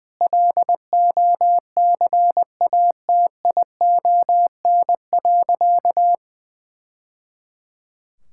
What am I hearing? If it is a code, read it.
Morse code, heard as a keyed tone, about 20 wpm, 700 Hz. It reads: LOCATION.
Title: Sentences from Top 1000 Words - Review Part 6 - 20wpm